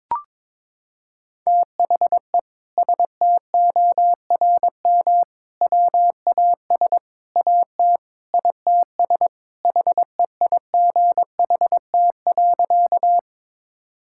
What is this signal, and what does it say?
Morse code, heard as a keyed tone, about 21 wpm, 700 Hz. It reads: THE STORM WAS AT ITS HEIGHT.
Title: Sentences from Top 1600 Words 22wpm